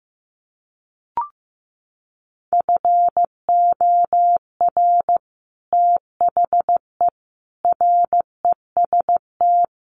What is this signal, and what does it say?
Morse code, heard as a keyed tone, about 15 wpm, 700 Hz. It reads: FOR THE REST
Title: Sets of 5 Words 15wpm